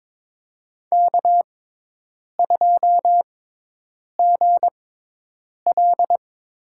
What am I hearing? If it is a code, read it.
Morse code, heard as a keyed tone, about 22 wpm, spaced as for 8 wpm, 700 Hz. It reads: K2GL
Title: USA Call Signs - 6x Character Spacing - 22wpm